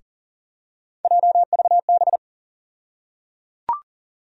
Morse code, heard as a keyed tone, about 37 wpm, 700 Hz. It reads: JVB
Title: USA Call Sign Postfixes - 3 Letters 40wpm